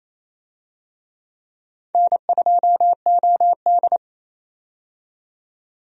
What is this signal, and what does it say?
Morse code, heard as a keyed tone, about 28 wpm, 700 Hz. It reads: N2OD
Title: USA Call Signs - 4 Characters - 28wpm